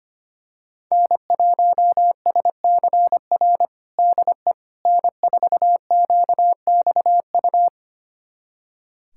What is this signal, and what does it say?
Morse code, heard as a keyed tone, about 25 wpm, 700 Hz. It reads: N1SCR DE N4QXU
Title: USA Callsign DE Callsign - 5 Characters - 25wpm